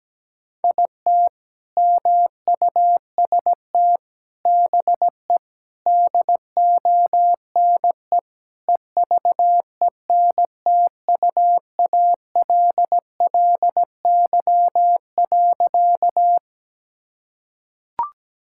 Morse code, heard as a keyed tone, about 17 wpm, 700 Hz. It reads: IT MUST BE DONE EVENTUALLY.
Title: Sentences from Top 1000 Words - Review Part 6 - 17wpm